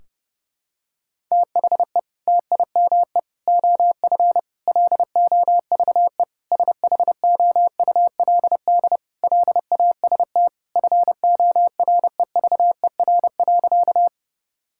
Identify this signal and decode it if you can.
Morse code, heard as a keyed tone, about 30 wpm, 700 Hz. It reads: THE TIME OF LOVE SHOULD LAST FOREVER.
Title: Sentences from Top 1600 Words 30wpm